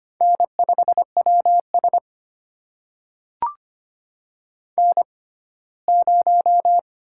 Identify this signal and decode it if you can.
Morse code, heard as a keyed tone, about 25 wpm, spaced as for 17 wpm, 700 Hz. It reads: N5WS N0
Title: USA Call Signs - 6x Character Spacing - 25wpm